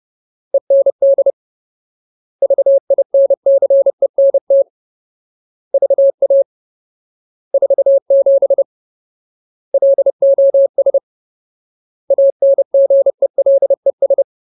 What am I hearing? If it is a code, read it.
Morse code, heard as a keyed tone, about 30 wpm, 550 Hz. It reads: END VINCENT VA 47 LOS ANGELES
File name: Ultimate Word List / 30WPM Ultimate